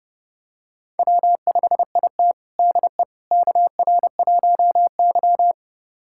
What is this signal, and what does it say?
Morse code, heard as a keyed tone, about 31 wpm, 700 Hz. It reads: W5IT DE KR1Y
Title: USA Callsign DE Callsign - 4 Characters - 30wpm